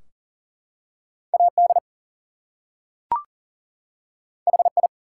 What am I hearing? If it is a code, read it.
Morse code, heard as a keyed tone, about 40 wpm, 700 Hz. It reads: AD HI
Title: Two Letter Words - 40wpm